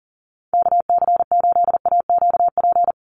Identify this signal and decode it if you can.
Morse code, heard as a keyed tone, about 40 wpm, 700 Hz. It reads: KC8AQP
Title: USA Call Signs 40wpm